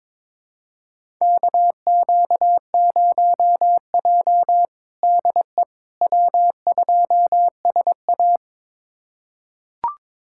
Morse code, heard as a keyed tone, about 22 wpm, 700 Hz. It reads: KQ0J DE W2SA
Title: USA Callsign DE Callsign - 4 Characters - 22wpm